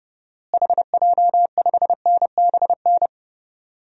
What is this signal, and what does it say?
Morse code, heard as a keyed tone, about 30 wpm, 700 Hz. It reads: HJ5NBN
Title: DXCC-100 Call Signs - 6 Characters - Repeated 2x - 30wpm